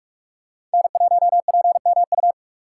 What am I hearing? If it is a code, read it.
Morse code, heard as a keyed tone, about 46 wpm, 700 Hz. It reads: N1PMU
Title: USA Call Signs - 2x Character Spacing - 45wpm